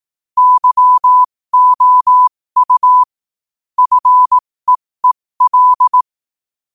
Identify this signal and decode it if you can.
Morse code, heard as a keyed tone, about 18 wpm, spaced as for 15 wpm, 1000 Hz. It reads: YOU FEEL
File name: qotdfiles / 15wpm